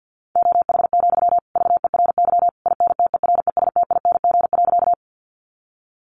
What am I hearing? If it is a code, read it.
Morse code, heard as a keyed tone, about 50 wpm, 700 Hz. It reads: OH, VERY INTERESTING.
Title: Sentences from Top 1000 Words 50wpm